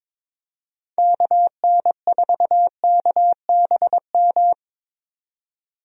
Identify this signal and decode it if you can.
Morse code, heard as a keyed tone, about 22 wpm, 700 Hz. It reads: KN4KBM
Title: USA Call Signs - 6 Characters - 22wpm